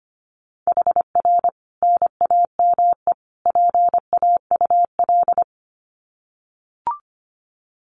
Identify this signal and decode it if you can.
Morse code, heard as a keyed tone, about 25 wpm, 700 Hz. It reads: HR NAME PAUL
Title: Common QSO Phrases 25wpm